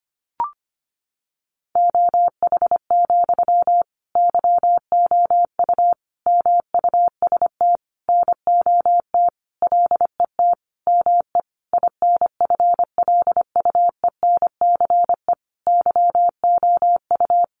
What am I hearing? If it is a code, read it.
Morse code, heard as a keyed tone, about 25 wpm, 700 Hz. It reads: OH, YOU MUST NOT LET ME INFLUENCE YOU
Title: Sentences from Top 1100 Words 25wpm